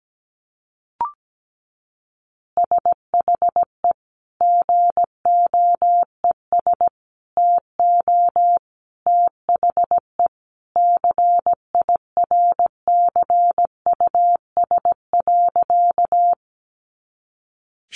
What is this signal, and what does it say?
Morse code, heard as a keyed tone, about 17 wpm, 700 Hz. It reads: SHE GOES TO THE CIRCUS.
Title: Easy Phrases 17wpm